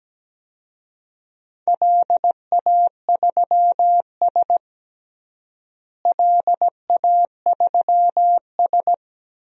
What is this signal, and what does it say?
Morse code, heard as a keyed tone, about 17 wpm, 700 Hz. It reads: LA3S LA3S
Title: DXCC-100 Call Signs - 4 Characters - Repeated 2x - 17wpm